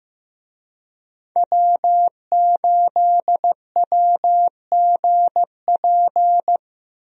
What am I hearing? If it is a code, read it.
Morse code, heard as a keyed tone, about 15 wpm, 700 Hz. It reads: W8WGP
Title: USA Call Signs - 5 Characters - Repeated 3x - 15wpm